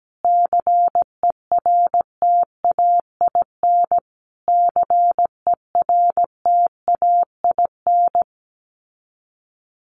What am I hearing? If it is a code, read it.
Morse code, heard as a keyed tone, about 17 wpm, 700 Hz. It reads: CERTAIN CERTAIN
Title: Top 500 Words Repeated 17wpm